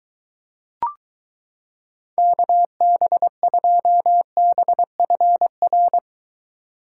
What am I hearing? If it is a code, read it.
Morse code, heard as a keyed tone, about 23 wpm, 700 Hz. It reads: KB2BFR